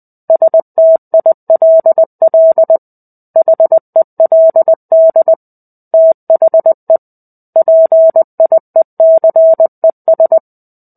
Morse code, heard as a keyed tone, about 20 wpm, 650 Hz. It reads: STILL HELD THE PIECES
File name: 05-Chapter-05 20 WPM